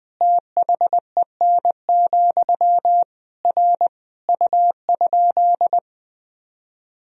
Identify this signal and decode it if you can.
Morse code, heard as a keyed tone, about 20 wpm, 700 Hz. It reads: THEN, R U?
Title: Sentences from Top 100 Words and Abbreviations 20wpm